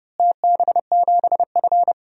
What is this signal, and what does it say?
Morse code, heard as a keyed tone, about 30 wpm, 700 Hz. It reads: TB7F
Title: DXCC-100 Call Signs - 4 Characters - Repeated 3x - 30wpm